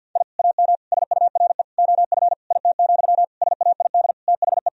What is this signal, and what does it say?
Morse code, heard as a keyed tone, about 50 wpm, 700 Hz. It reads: I AM SURE OF IT, SAID THE
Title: Sentences from Top 1000 Words - Review Part 4 - 50wpm